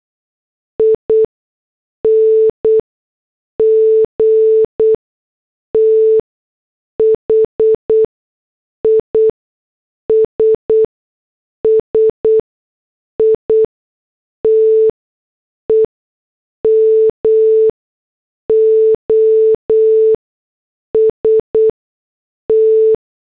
Morse code, heard as a keyed tone, about 8 wpm, spaced as for 6 wpm, 440 Hz. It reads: ING THIS SITE MOST